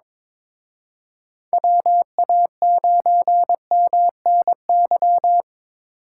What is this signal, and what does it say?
Morse code, heard as a keyed tone, about 26 wpm, 700 Hz. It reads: WA9MNY
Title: Speed Racing - USA 2x3 Call Signs 22wpm